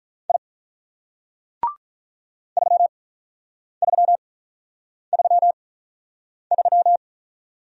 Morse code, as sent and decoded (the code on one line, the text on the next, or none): .. / ...-- ...-- ...-- ...--
I 3333